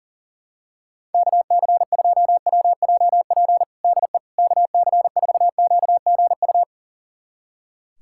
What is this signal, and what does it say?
Morse code, heard as a keyed tone, about 40 wpm, 700 Hz. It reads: KC2WJP DE KC4QGU
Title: USA Callsign DE Callsign - 6 Characters - 40wpm